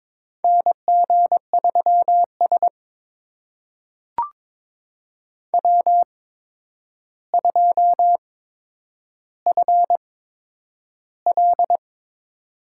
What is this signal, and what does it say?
Morse code, heard as a keyed tone, about 22 wpm, spaced as for 9 wpm, 700 Hz. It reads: NG3S W2FL
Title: USA Call Signs - 4 Characters - 8x Character Spacing - 22wpm